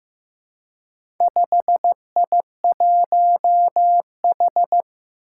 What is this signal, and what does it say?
Morse code, heard as a keyed tone, about 15 wpm, 700 Hz. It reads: 5I1H
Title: Four Challenging Letter-Number Combinations 15wpm